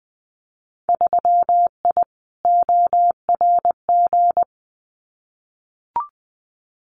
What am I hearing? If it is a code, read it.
Morse code, heard as a keyed tone, about 20 wpm, 700 Hz. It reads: 3I ORG
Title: Winter Field Day Exchanges Encore 20wpm